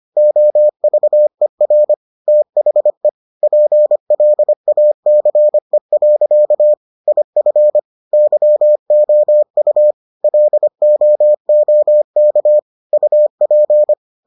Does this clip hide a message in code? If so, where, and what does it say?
Yes: Morse code from 0.2 to 13.9 s, heard as a keyed tone, about 25 wpm, 600 Hz. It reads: OVER THE PLACE. IF YOU LOOK UP